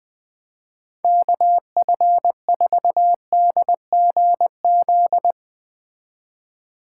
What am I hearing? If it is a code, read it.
Morse code, heard as a keyed tone, about 20 wpm, 700 Hz. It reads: KF4DGZ
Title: Speed Racing - USA 2x3 Call Signs 17wpm